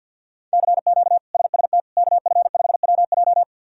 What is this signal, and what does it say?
Morse code, heard as a keyed tone, about 51 wpm, 700 Hz. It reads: CQ SST KF5WJ
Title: K1USN SST QSO Practice - 50wpm